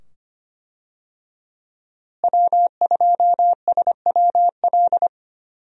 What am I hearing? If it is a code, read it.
Morse code, heard as a keyed tone, about 25 wpm, 700 Hz. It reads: W2SWL
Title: USA Call Signs - 5 Characters - Repeated 2x - 25wpm